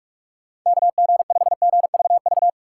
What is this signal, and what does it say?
Morse code, heard as a keyed tone, about 58 wpm, 700 Hz. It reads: KG5GVV